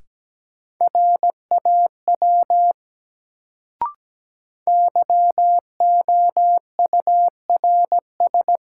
Morse code, heard as a keyed tone, about 17 wpm, 700 Hz. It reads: RAW YOURS
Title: Sentences from Top 1500 Words - New Word Drill Alt - 17wpm